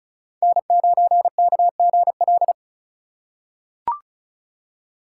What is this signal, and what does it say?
Morse code, heard as a keyed tone, about 36 wpm, 700 Hz. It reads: N9KGL
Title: USA Call Signs - 8x Character Spacing - 35wpm